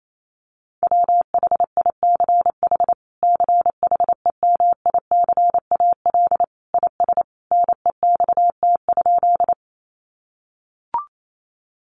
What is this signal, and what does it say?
Morse code, heard as a keyed tone, about 28 wpm, 700 Hz. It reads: WHICH CHEMICAL IS NEXT?
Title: Sentences from Top 1200 Words 28wpm